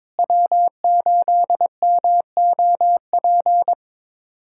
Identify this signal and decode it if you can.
Morse code, heard as a keyed tone, about 22 wpm, 700 Hz. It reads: W8MOP
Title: USA Callsign DE Callsign - 5 Characters - 22wpm